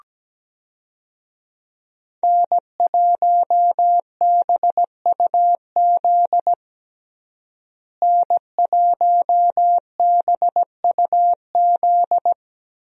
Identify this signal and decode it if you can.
Morse code, heard as a keyed tone, about 17 wpm, 700 Hz. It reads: N1BUZ N1BUZ
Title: USA Call Signs - 5 Characters - Repeated 2x - 17wpm